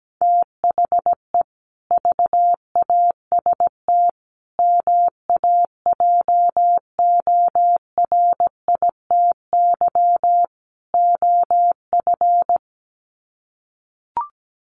Morse code, heard as a keyed tone, about 17 wpm, 700 Hz. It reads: THE VAST MAJORITY OF
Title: Sets of 4 Words 17wpm